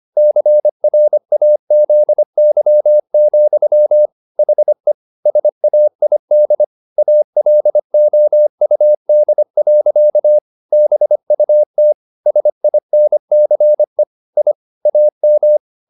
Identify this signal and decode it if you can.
Morse code, heard as a keyed tone, about 25 wpm, 600 Hz. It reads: CRAZY, HE SAID ALOUD. BUT SINCE I AM
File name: the old man and the sea part 1 25wpm